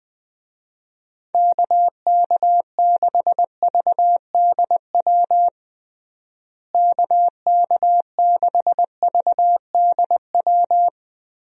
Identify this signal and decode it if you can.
Morse code, heard as a keyed tone, about 20 wpm, 700 Hz. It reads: KK6VDW KK6VDW
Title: USA Call Signs - 6 Characters - Repeated 3x - 20wpm